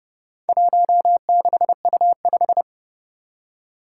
Morse code, heard as a keyed tone, about 30 wpm, 700 Hz. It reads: 16U5
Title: Four Challenging Letter-Number Combinations 30wpm